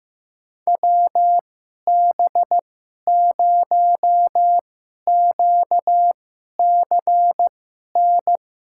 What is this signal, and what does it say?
Morse code, heard as a keyed tone, about 16 wpm, spaced as for 11 wpm, 700 Hz. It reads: WB0QCN
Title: USA Call Signs - 6 Characters - 2x Character Spacing - 15wpm